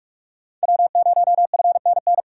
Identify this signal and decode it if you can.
Morse code, heard as a keyed tone, about 48 wpm, 700 Hz. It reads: W0FNN
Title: USA Call Signs - 5 Characters - 6x Character Spacing - 45wpm